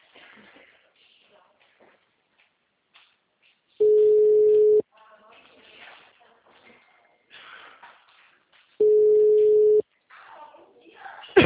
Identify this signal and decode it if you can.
Ringback tone (Europe, 425 Hz), with speech.
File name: out- -2004- - - .440